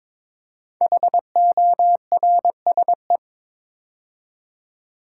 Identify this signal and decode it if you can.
Morse code, heard as a keyed tone, about 22 wpm, 700 Hz. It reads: HORSE
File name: 5 Letter Words - First 3 Letters Given - 22wpm